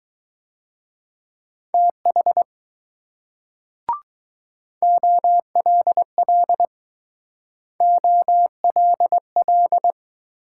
Morse code, heard as a keyed tone, about 19 wpm, 700 Hz. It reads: TH OLL OLL